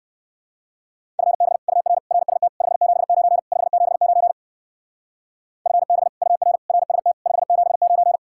Morse code, heard as a keyed tone, about 59 wpm, 700 Hz. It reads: FB UR RST 579 579 FB UR RST 579